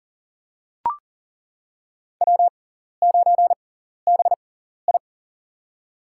Morse code, heard as a keyed tone, about 40 wpm, spaced as for 14 wpm, 700 Hz. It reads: W9BI
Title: USA Call Signs - 6x Character Spacing - 40wpm